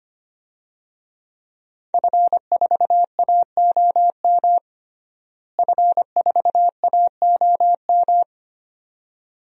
Morse code, heard as a keyed tone, about 25 wpm, 700 Hz. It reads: F4AOM F4AOM
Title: DXCC-100 Call Signs - 5 Characters - Repeated 2x - 25wpm